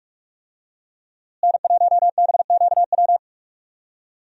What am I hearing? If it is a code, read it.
Morse code, heard as a keyed tone, about 41 wpm, 700 Hz. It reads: N1BQW
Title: USA Call Signs - 5 Characters - Repeated 2x - 45wpm